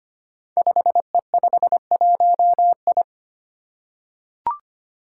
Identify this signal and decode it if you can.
Morse code, heard as a keyed tone, about 25 wpm, 700 Hz. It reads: 5E51I
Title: Five Challenging Letter-Number Combinations 25wpm